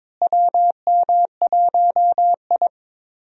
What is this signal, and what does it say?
Morse code, heard as a keyed tone, about 22 wpm, 700 Hz. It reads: WM1I
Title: USA Call Signs - 4 Characters - Repeated 3x - 22wpm